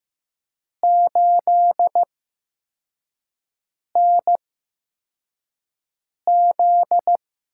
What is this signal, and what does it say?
Morse code, heard as a keyed tone, about 15 wpm, spaced as for 4 wpm, 700 Hz. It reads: 8NZ
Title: USA Call Signs - 6 Characters - 8x Character Spacing - 15wpm